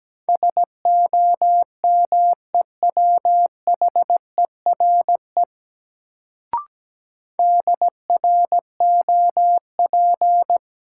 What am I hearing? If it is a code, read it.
Morse code, heard as a keyed tone, about 17 wpm, 700 Hz. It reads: SOMEWHERE DROP
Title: Sentences from Top 1400 Words - New Word Drill Alt - 17wpm